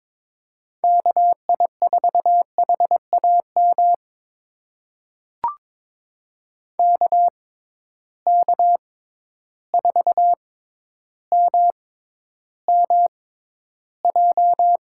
Morse code, heard as a keyed tone, about 22 wpm, spaced as for 11 wpm, 700 Hz. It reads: KI4HAM KK4MMJ
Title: USA Call Signs - 6 Characters - 6x Character Spacing - 22wpm